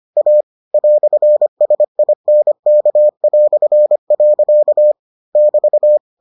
Morse code, heard as a keyed tone, about 25 wpm, 600 Hz. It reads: A "SINK". <BT>